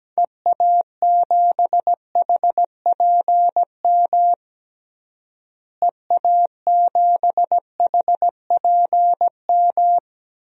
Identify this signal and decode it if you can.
Morse code, heard as a keyed tone, about 17 wpm, 700 Hz. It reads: EA7HPM EA7HPM
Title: DXCC-100 Call Signs - 6 Characters - Repeated 3x - 17wpm